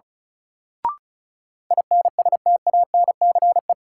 Morse code, heard as a keyed tone, about 36 wpm, 700 Hz. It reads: INSTANCE
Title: Sentences from Top 800 Words - New Word Drill - 35wpm